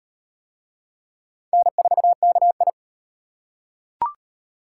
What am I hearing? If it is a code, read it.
Morse code, heard as a keyed tone, about 35 wpm, 700 Hz. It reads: N4KI